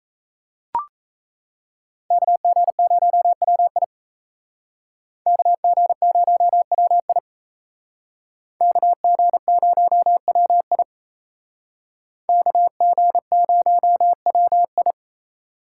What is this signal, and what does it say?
Morse code, heard as a keyed tone, about 37 wpm, 700 Hz. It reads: KG0WI KG0WI KG0WI KG0WI